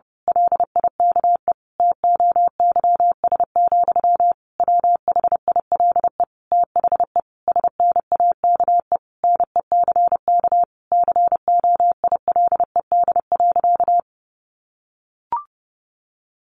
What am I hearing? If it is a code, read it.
Morse code, heard as a keyed tone, about 30 wpm, 700 Hz. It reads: LIKE TOYS, WHILE THE SNAKE NECK COILED.
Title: A Sound Of Thunder 30wpm